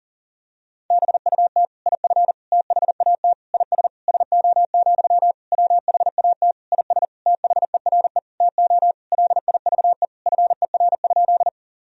Morse code, heard as a keyed tone, about 40 wpm, 700 Hz. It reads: BUT IF THAT IS SO, WHAT IS THERE TO LIVE FER?
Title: Sentences from Top 200 Words and Abbreviations 40wpm